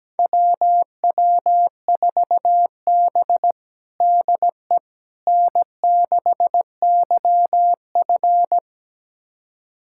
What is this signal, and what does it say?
Morse code, heard as a keyed tone, about 17 wpm, 700 Hz. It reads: WW4B DE N6YF
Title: USA Callsign DE Callsign - 4 Characters - 17wpm